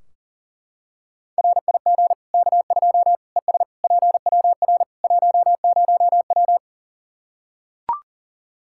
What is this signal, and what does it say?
Morse code, heard as a keyed tone, about 37 wpm, 700 Hz. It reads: RIG K2 ES PWR 10W
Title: Practice QSO - Exchange 2 - Info - 40wpm